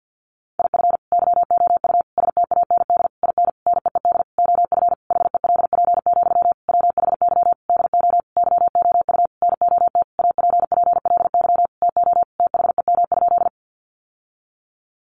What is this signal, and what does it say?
Morse code, heard as a keyed tone, about 50 wpm, 700 Hz. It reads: IF YOU STAND IN NEED OF HELP, WHY DO YOU NOT APPLY TO THEM?
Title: Sentences from Top 1200 Words 50wpm